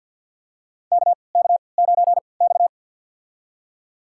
Morse code, heard as a keyed tone, about 53 wpm, spaced as for 29 wpm, 700 Hz. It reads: KK9X
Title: USA Call Signs - 4 Characters - 3x Character Spacing - 50wpm